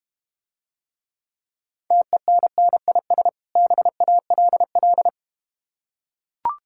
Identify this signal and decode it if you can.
Morse code, heard as a keyed tone, about 32 wpm, 690 Hz. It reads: TENNIS BALL